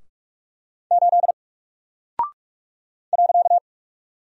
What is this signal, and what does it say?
Morse code, heard as a keyed tone, about 44 wpm, 700 Hz. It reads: 8 .
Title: Single Character (Letter:Number:Punctuation:Prosign) 45wpm